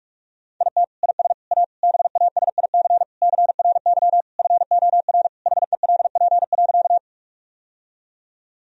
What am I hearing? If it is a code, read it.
Morse code, heard as a keyed tone, about 41 wpm, 700 Hz. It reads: IT IS A BASIC CRY FOR HELP.
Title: Sentences from Top 1000 Words - Review Part 3 - 45wpm